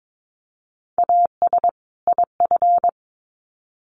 Morse code, heard as a keyed tone, about 22 wpm, 700 Hz. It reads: AS IF
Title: Sets of 2 Words 22wpm